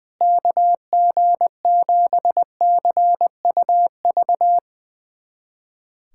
Morse code, heard as a keyed tone, about 20 wpm, 700 Hz. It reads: KG7CUV
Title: USA Callsign DE Callsign - 6 Characters - 20wpm